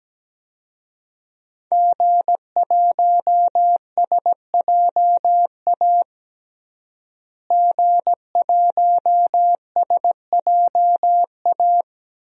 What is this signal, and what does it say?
Morse code, heard as a keyed tone, about 17 wpm, 700 Hz. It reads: G1SJA G1SJA
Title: DXCC-100 Call Signs - 5 Characters - Repeated 2x - 17wpm